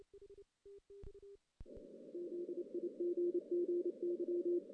Morse code, heard as a keyed tone, about 28 wpm, 380 Hz. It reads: ZIGGY
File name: 2016 JAN CW NAQP / 2016 Jan naqp k5ot - 0-11